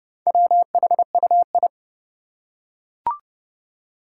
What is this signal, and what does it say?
Morse code, heard as a keyed tone, about 30 wpm, 700 Hz. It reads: WHUI
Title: Four Challenging Letter-Number Combinations 30wpm